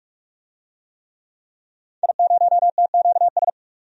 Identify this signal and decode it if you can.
Morse code, heard as a keyed tone, about 42 wpm, 700 Hz. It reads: I0TQS
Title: DXCC-100 Call Signs - 5 Characters - 45wpm